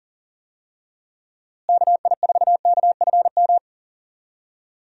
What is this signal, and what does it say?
Morse code, heard as a keyed tone, about 40 wpm, 700 Hz. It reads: KI4KFM
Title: USA Call Signs - 6 Characters - Repeated 3x - 40wpm